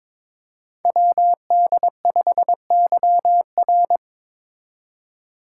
Morse code, heard as a keyed tone, about 22 wpm, 700 Hz. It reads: WD5YR
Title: Code Groups - 5 Letter:Numbers - 22wpm